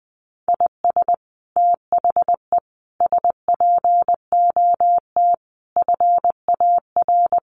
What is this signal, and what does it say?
Morse code, heard as a keyed tone, about 20 wpm, 700 Hz. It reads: IS THE SPOT FAR
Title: Sentences from Top 1100 Words 20wpm